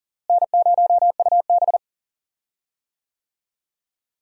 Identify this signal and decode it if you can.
Morse code, heard as a keyed tone, about 40 wpm, 700 Hz. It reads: N0UB
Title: USA Call Signs - 4 Characters - Repeated 2x - 40wpm